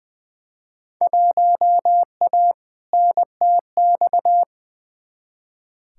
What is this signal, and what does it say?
Morse code, heard as a keyed tone, about 21 wpm, 700 Hz. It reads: 1A NTX
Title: Field Day Exchanges - 20wpm